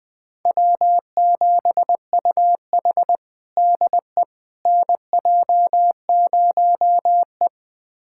Morse code, heard as a keyed tone, about 21 wpm, 700 Hz. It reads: W7UH DE NJ0E
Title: USA Callsign DE Callsign - 4 Characters - 20wpm